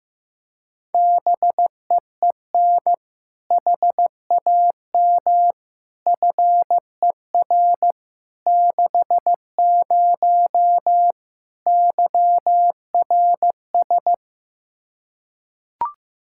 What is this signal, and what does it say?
Morse code, heard as a keyed tone, about 15 wpm, 700 Hz. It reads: BEEN HAM FER 60 YRS
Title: Practice QSO - Exchange 3 - More Info - 15wpm